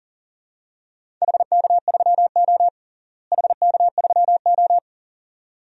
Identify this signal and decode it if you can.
Morse code, heard as a keyed tone, about 40 wpm, 700 Hz. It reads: HK3O HK3O
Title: DXCC-100 Call Signs - 4 Characters - Repeated 2x - 40wpm